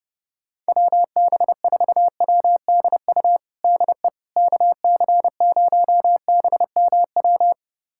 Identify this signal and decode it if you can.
Morse code, heard as a keyed tone, about 30 wpm, 700 Hz. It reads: WB4WDU DE KC0BMW
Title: USA Callsign DE Callsign - 6 Characters - 30wpm